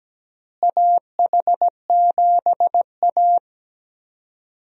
Morse code, heard as a keyed tone, about 17 wpm, 700 Hz. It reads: AH7A
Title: USA Call Signs - 4 Characters - Repeated 2x - 17wpm